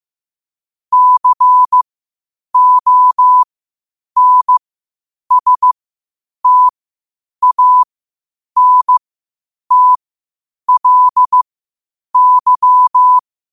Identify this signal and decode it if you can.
Morse code, heard as a keyed tone, about 15 wpm, spaced as for 8 wpm, 1000 Hz. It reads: CONSTANTLY
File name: qotdfiles / 8wpm